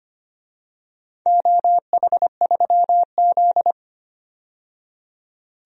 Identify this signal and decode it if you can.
Morse code, heard as a keyed tone, about 25 wpm, 700 Hz. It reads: OH3Z
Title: DXCC-100 Call Signs - 4 Characters - 25wpm